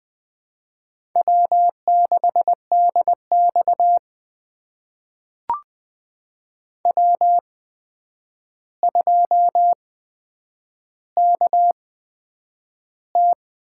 Morse code, heard as a keyed tone, about 20 wpm, spaced as for 9 wpm, 700 Hz. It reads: W6DX W2KT
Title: USA Call Signs - 4 Characters - 8x Character Spacing - 20wpm